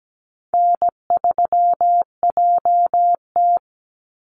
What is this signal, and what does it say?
Morse code, heard as a keyed tone, about 17 wpm, 700 Hz. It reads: N3JT
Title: Easy Call Signs - 17wpm